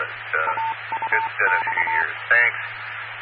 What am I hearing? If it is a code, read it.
Morse code, heard as a keyed tone, about 24 wpm, 900 Hz, with speech. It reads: AUX